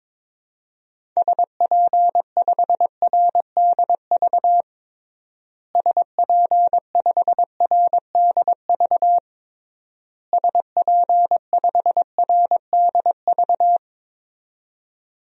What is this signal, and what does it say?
Morse code, heard as a keyed tone, about 22 wpm, 700 Hz. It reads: SP5RDV SP5RDV SP5RDV
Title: DXCC-100 Call Signs - 6 Characters - Repeated 3x - 22wpm